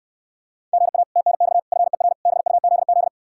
Morse code, heard as a keyed tone, about 67 wpm, 700 Hz. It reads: GA TNX FER BUZZ